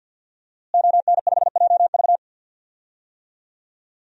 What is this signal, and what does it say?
Morse code, heard as a keyed tone, about 52 wpm, 700 Hz. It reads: ON5JV
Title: DXCC-100 Call Signs - 5 Characters - Repeated 2x - 50wpm